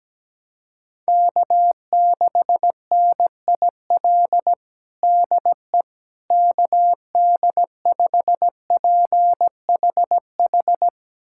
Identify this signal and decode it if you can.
Morse code, heard as a keyed tone, about 17 wpm, 700 Hz. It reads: K6NIL DE KD5PHH
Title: USA Callsign DE Callsign 17wpm